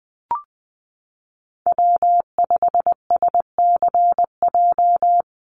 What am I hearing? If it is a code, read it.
Morse code, heard as a keyed tone, about 20 wpm, 700 Hz. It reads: W5SCJ
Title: USA Call Signs Repeated 2x 20wpm